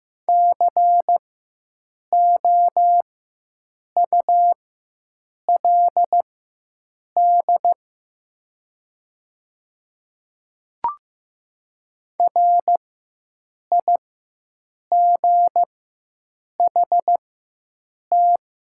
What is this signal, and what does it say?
Morse code, heard as a keyed tone, about 15 wpm, spaced as for 7 wpm, 700 Hz. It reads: COULD RIGHT
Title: 4x Character Spacing - Top 200 Words - 15wpm